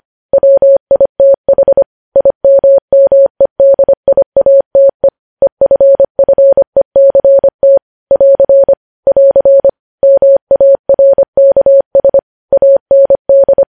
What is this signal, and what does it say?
Morse code, heard as a keyed tone, about 25 wpm, 550 Hz. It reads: WITH IMMEDIATE EFFECT <AR> <AR> MARKS AND
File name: podcast / News headlines-1x-25x25